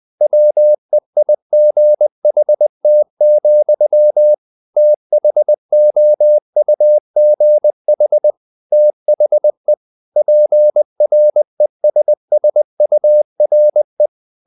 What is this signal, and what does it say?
Morse code, heard as a keyed tone, about 20 wpm, 600 Hz. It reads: WEIGHT, THOUGH THE PRESSURE
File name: the old man and the sea / the old man and the sea part 1 20wpm